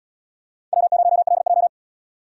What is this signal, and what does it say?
Morse code, heard as a keyed tone, about 72 wpm, 700 Hz. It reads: W0LJ